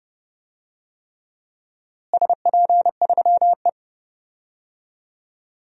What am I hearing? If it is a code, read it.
Morse code, heard as a keyed tone, about 30 wpm, 700 Hz. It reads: SP3E
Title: DXCC-100 Call Signs - 4 Characters - Repeated 3x - 30wpm